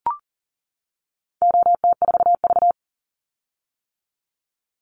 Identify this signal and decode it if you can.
Morse code, heard as a keyed tone, about 40 wpm, 700 Hz. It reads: OT4V
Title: Call Signs - 40wpm